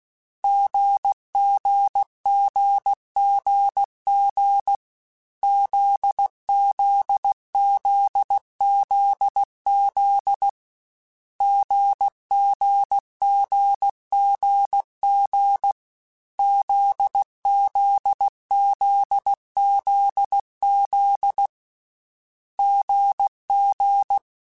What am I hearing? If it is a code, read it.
Morse code, heard as a keyed tone, about 16 wpm, 800 Hz. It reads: GGGGG ZZZZZ GGGGG ZZZZZ GG
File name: Difficult Combos / Difficult Combos-mp3s / GZ GZ7 @15